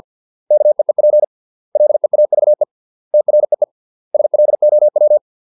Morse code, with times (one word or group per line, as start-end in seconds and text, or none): 0.5-1.2 KEEP
1.8-2.6 LEAVE
3.1-3.6 TREE
4.1-5.2 SLOW